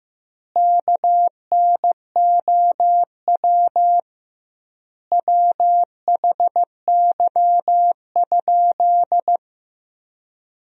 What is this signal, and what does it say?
Morse code, heard as a keyed tone, about 15 wpm, 700 Hz. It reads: KNOW WHY?
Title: 2x Word-Spacing - Sentences from Top 200 Words - 15wpm